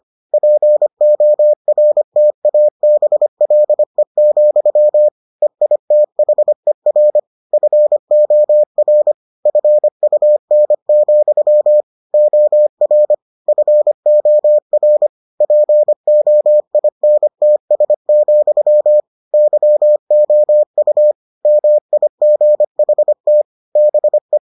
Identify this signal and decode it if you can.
Morse code, heard as a keyed tone, about 25 wpm, 600 Hz. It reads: PORTABLE, EITHER FOR FUN, OR FOR POINTS, YOU MIGHT BE